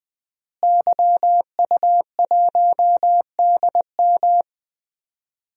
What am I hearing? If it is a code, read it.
Morse code, heard as a keyed tone, about 20 wpm, 700 Hz. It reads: YU1DM
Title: DXCC-100 Call Signs - 5 Characters - Repeated 3x - 20wpm